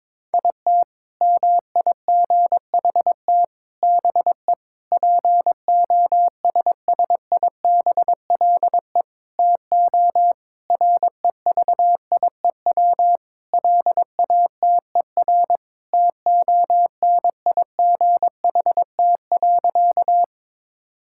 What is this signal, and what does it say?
Morse code, heard as a keyed tone, about 22 wpm, 700 Hz. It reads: IT MIGHT BE POSSIBLE TO REVIEW LATER TONIGHT.
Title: Sentences from Top 1000 Words - Review Part 6 - 22wpm